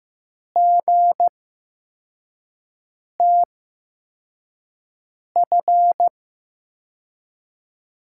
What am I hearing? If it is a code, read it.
Morse code, heard as a keyed tone, about 15 wpm, spaced as for 4 wpm, 700 Hz. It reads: GTF
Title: USA Call Signs - 5 Characters - 8x Character Spacing - 15wpm